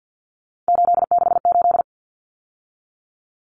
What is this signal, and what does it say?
Morse code, heard as a keyed tone, about 50 wpm, 700 Hz. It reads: 868
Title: Numbers - 3-Digit 50wpm